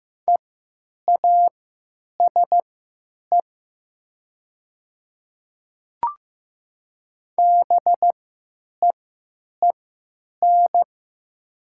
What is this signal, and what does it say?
Morse code, heard as a keyed tone, about 15 wpm, spaced as for 9 wpm, 700 Hz. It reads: EASE BEEN
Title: 3x Character Spacing - Top 500 Words - 15wpm